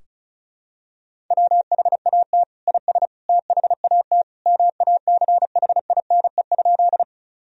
Morse code, heard as a keyed tone, about 36 wpm, 700 Hz. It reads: WHAT IS THAT MACHINE?
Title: Sentences from Top 1000 Words - Review Part 2 - 35wpm